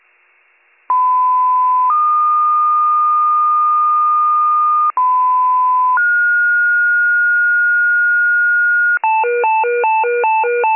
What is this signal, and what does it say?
Two-tone page, heard as a one-second tone, then a three-second tone, about 1010 then 1220 Hz, about 1010 then 1430 Hz, with noise. It is heard from 0.9 to 9.0 s.